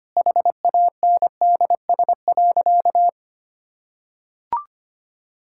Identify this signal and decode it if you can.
Morse code, heard as a keyed tone, about 25 wpm, 700 Hz. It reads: HANDS.
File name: practice / 25wpm / Sentences with 2-Word Phrases - 25wpm